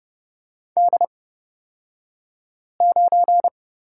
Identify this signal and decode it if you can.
Morse code, heard as a keyed tone, about 30 wpm, spaced as for 5 wpm, 700 Hz. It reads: D9
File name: Lesson 38d - Letter q - Two Characters 30wpm